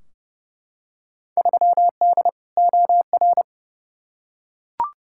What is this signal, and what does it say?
Morse code, heard as a keyed tone, about 32 wpm, 700 Hz. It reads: 3D OR
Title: Field Day Exchanges Encore - 30wpm